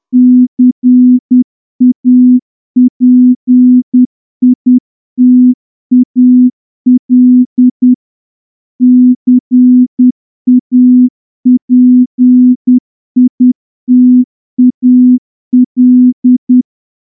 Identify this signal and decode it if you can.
Morse code, heard as a keyed tone, about 10 wpm, 250 Hz. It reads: CAPITAL CAPITAL
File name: morse-training-10WPM